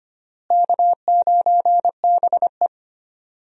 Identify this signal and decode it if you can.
Morse code, heard as a keyed tone, about 25 wpm, 700 Hz. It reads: K9BE
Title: USA Call Signs - 4 Characters - Repeated 3x - 25wpm